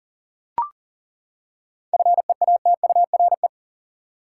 Morse code, heard as a keyed tone, about 40 wpm, 700 Hz. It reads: FEATURE
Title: Seven Letter Words - 40wpm